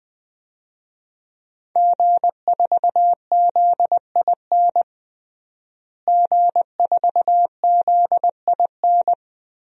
Morse code, heard as a keyed tone, about 21 wpm, 700 Hz. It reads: G4ZIN G4ZIN
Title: DXCC-100 Call Signs - 5 Characters - Repeated 2x - 20wpm